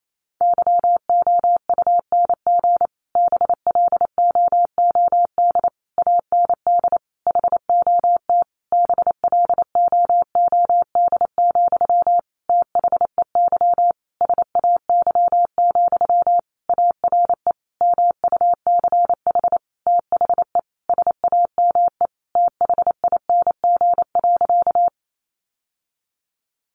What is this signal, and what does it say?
Morse code, heard as a keyed tone, about 28 wpm, 700 Hz. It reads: YOUNG BLOOD AND HOT BLOOD, THEY SAY, ARE MUCH THE SAME THING.
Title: Sentences from Top 1100 Words 28wpm